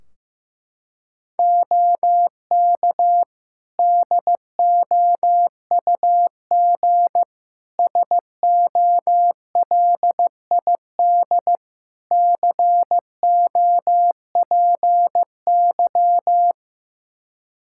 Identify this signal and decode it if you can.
Morse code, heard as a keyed tone, about 15 wpm, 700 Hz. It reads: OK DOUG SOLID COPY
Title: K1USN SST QSO Practice 15wpm